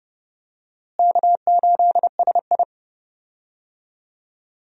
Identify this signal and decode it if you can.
Morse code, heard as a keyed tone, about 30 wpm, 700 Hz. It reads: K8SI
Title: USA Call Signs - 4 Characters - Repeated 3x - 30wpm